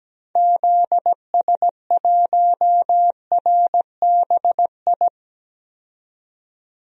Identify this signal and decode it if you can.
Morse code, heard as a keyed tone, about 17 wpm, 700 Hz. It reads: ZS1RBI
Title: DXCC-100 Call Signs - 6 Characters - 17wpm